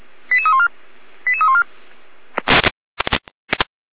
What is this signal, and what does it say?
Five-tone selective call (ZVEI), heard as quick runs of five tones, tones of about 80 ms. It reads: 88314 88314